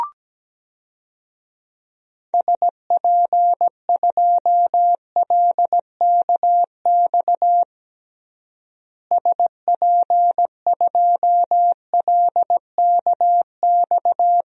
Morse code, heard as a keyed tone, about 17 wpm, 700 Hz. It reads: SP2LKX SP2LKX
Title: DXCC-100 Call Signs - 6 Characters - Repeated 3x - 17wpm